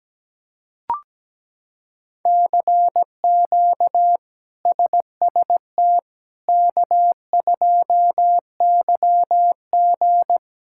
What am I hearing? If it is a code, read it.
Morse code, heard as a keyed tone, about 17 wpm, 700 Hz. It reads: CQ SST K2YG
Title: K1USN SST QSO Practice - 17wpm